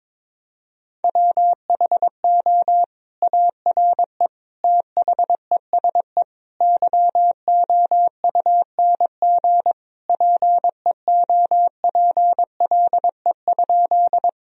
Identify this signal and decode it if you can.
Morse code, heard as a keyed tone, about 22 wpm, 700 Hz. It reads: WHO ARE THESE YOUNG PEOPLE?
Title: Sentences from Top 1000 Words - Review Part 1 - 22wpm